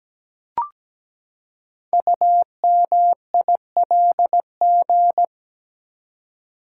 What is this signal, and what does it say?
Morse code, heard as a keyed tone, about 17 wpm, 700 Hz. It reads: UMILG
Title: Code Groups - 5 Letters - 17wpm